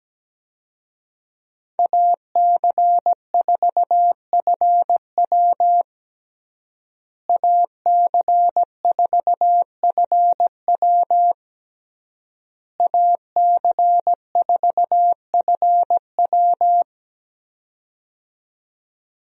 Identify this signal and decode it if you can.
Morse code, heard as a keyed tone, about 17 wpm, 700 Hz. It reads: AC4FW AC4FW AC4FW
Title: USA Call Signs - 5 Characters - Repeated 3x - 17wpm